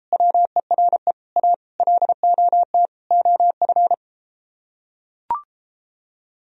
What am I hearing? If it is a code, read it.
Morse code, heard as a keyed tone, about 33 wpm, 700 Hz. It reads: WERE A LOT OF